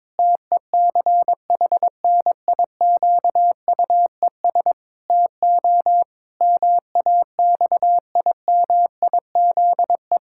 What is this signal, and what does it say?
Morse code, heard as a keyed tone, about 22 wpm, 700 Hz. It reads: TECHNIQUES TO MAXIMIZE
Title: Pangram Challenge v2 - 22wpm